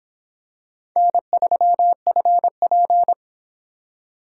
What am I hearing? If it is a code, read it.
Morse code, heard as a keyed tone, about 26 wpm, 700 Hz. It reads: N3FP